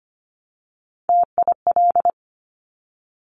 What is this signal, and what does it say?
Morse code, heard as a keyed tone, about 25 wpm, 700 Hz. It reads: TIL
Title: Top Three Letter Combinations in Words 25wpm